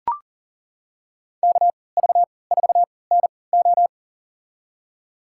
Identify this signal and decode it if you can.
Morse code, heard as a keyed tone, about 40 wpm, spaced as for 23 wpm, 690 Hz. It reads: KV4NO
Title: USA Call Signs - 5 Characters - 3x Character Spacing - 40wpm